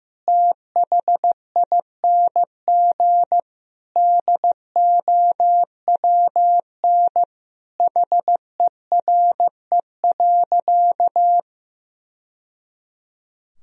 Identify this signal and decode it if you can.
Morse code, heard as a keyed tone, about 16 wpm, 700 Hz. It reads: THING DOWN HERE.
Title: Sentences from Top 1000 Words - Review Part 3 - 15wpm